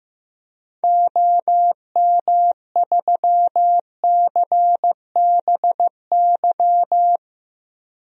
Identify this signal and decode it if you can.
Morse code, heard as a keyed tone, about 15 wpm, 700 Hz. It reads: OM3CBY
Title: DXCC-100 Call Signs - 6 Characters - 15wpm